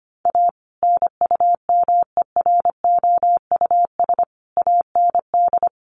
Morse code, heard as a keyed tone, about 25 wpm, 700 Hz. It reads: A NUMEROUS AND
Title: Sentences from Top 1400 Words 25wpm